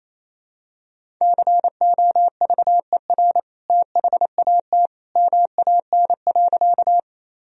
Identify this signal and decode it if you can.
Morse code, heard as a keyed tone, about 28 wpm, 700 Hz. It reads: COVER THAT MAN.
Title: Sentences from Top 1000 Words - Review Part 1 - 28wpm